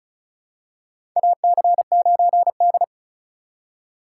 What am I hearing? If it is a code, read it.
Morse code, heard as a keyed tone, about 36 wpm, 700 Hz. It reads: AC9D
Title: USA Call Signs - 4x Character Spacing - 35wpm